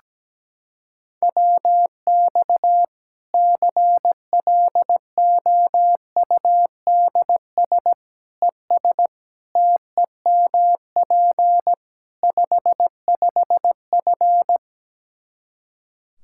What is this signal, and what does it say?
Morse code, heard as a keyed tone, about 17 wpm, 700 Hz. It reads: WX CLOUDS ES TEMP 55F
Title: K1USN SST QSO Practice 17wpm